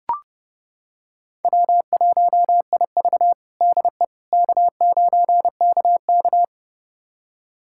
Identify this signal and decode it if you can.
Morse code, heard as a keyed tone, about 32 wpm, 700 Hz. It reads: W1IV DE K9KK
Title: USA Callsign DE Callsign - 4 Characters - 30wpm